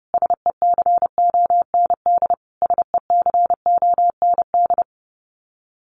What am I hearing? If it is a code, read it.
Morse code, heard as a keyed tone, about 30 wpm, 700 Hz. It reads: SECOND SECOND
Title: Top 500 Words Repeated 30wpm